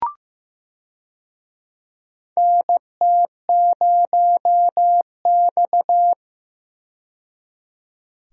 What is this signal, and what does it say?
Morse code, heard as a keyed tone, about 15 wpm, 700 Hz. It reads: NT0X
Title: USA Call Signs - 4 Characters - 15wpm